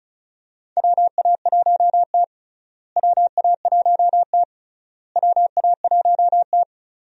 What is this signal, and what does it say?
Morse code, heard as a keyed tone, about 36 wpm, 700 Hz. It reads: WA1T WA1T WA1T
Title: USA Call Signs - 4 Characters - Repeated 3x - 35wpm